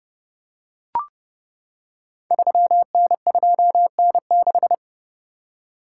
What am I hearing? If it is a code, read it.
Morse code, heard as a keyed tone, about 30 wpm, 700 Hz. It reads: 3N2N6
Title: Code Groups - 5 Letter:Numbers - 30wpm